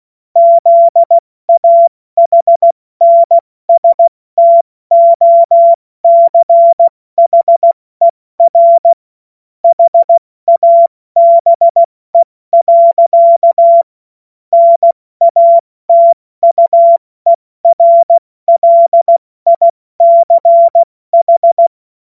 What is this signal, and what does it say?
Morse code, heard as a keyed tone, about 16 wpm, 680 Hz. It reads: ZAHNSTOCHER HABE. NATUERLICH
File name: Morsekurs / Witze / Witz 02 Januar 2024 70 F80